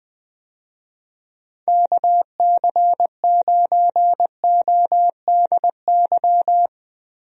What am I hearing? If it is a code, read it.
Morse code, heard as a keyed tone, about 20 wpm, 700 Hz. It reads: KC9ODY
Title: USA Call Signs - 6 Characters - Repeated 3x - 20wpm